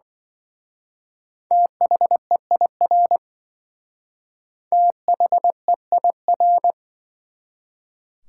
Morse code, heard as a keyed tone, about 24 wpm, 700 Hz. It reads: THEIR THEIR
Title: Speed-Racing Top 180 QSO Elements 20wpm